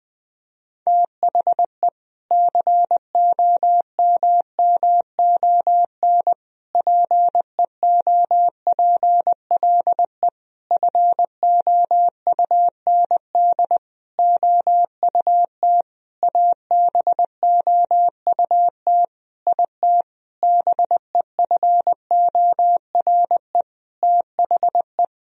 Morse code, heard as a keyed tone, about 20 wpm, 700 Hz. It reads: THE COMMON PEOPLE FOUND OUT ABOUT IT BEFORE THE
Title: Sentences from Top 1000 Words - Review Part 3 - 20wpm